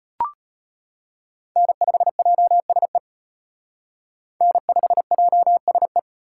Morse code, heard as a keyed tone, about 32 wpm, 700 Hz. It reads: N5JSE N5JSE